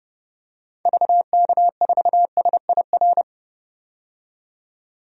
Morse code, heard as a keyed tone, about 30 wpm, 700 Hz. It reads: VK4SIR
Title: DXCC-100 Call Signs - 6 Characters - 30wpm